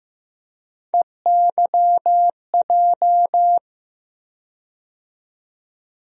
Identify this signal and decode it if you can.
Morse code, heard as a keyed tone, about 15 wpm, 700 Hz. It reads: EYJ
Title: USA Call Sign Postfixes - 3 Letters 15wpm